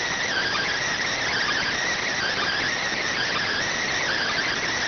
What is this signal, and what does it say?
Emergency Alert System (SAME) tones (header bursts), heard as harsh buzzing bursts. It is heard from 0.2 to 4.6 s.